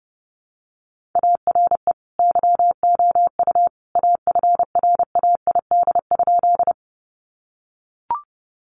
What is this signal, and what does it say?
Morse code, heard as a keyed tone, about 30 wpm, 700 Hz. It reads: ARE YOU AFRAID?
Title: Sentences from Top 1500 Words 30wpm